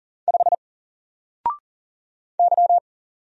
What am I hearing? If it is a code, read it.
Morse code, heard as a keyed tone, about 40 wpm, 700 Hz. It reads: 5 Y
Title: Single Character (Letter:Number:Punctuation:Prosign) 40wpm